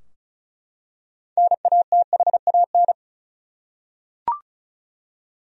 Morse code, heard as a keyed tone, about 34 wpm, 700 Hz. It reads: NATHAN
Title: USA Common Names - 35wpm